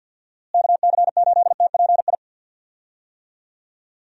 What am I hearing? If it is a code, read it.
Morse code, heard as a keyed tone, about 46 wpm, 700 Hz. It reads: KC8TPI
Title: USA Call Signs - 6 Characters - Repeated 2x - 50wpm